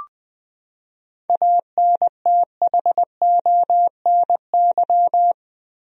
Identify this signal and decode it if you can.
Morse code, heard as a keyed tone, about 20 wpm, 700 Hz. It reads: ANTHONY
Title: USA Common Names - 20wpm